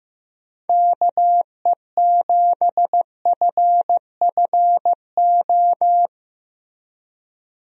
Morse code, heard as a keyed tone, about 15 wpm, 700 Hz. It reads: KE7FFO
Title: USA Call Signs - 6 Characters - Repeated 3x - 15wpm